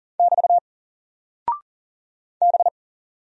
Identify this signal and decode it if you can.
Morse code, heard as a keyed tone, about 40 wpm, 700 Hz. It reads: <BT> B
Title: Single Character (Letter:Number:Punctuation:Prosign) Alt 40wpm